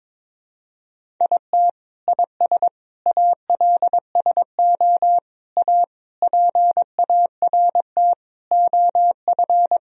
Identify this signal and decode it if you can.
Morse code, heard as a keyed tone, about 22 wpm, 700 Hz. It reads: IT IS ALSO A PART OF
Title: Sentences from Top 1000 Words - Review Part 1 - 22wpm